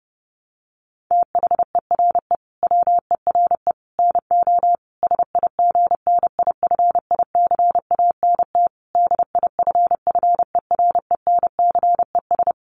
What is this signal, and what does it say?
Morse code, heard as a keyed tone, about 30 wpm, 700 Hz. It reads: THERE WERE NO SIGNIFICANT DIFFERENCES
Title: Sets of 5 Words Encore2 30wpm